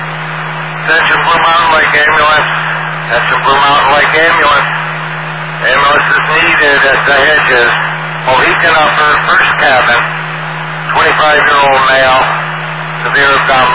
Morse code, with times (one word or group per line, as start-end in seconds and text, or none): none